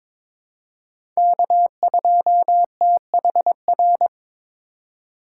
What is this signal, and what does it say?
Morse code, heard as a keyed tone, about 21 wpm, 700 Hz. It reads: K2THR
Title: USA Call Signs - 2x Character Spacing - 22wpm